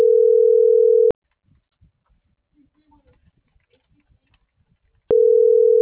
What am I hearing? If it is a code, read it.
Ringback tone (North America), 440 + 480 Hz.